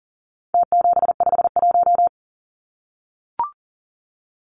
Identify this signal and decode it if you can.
Morse code, heard as a keyed tone, about 40 wpm, 700 Hz. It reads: T751
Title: Numbers - Contest Style - 4-Digit 40wpm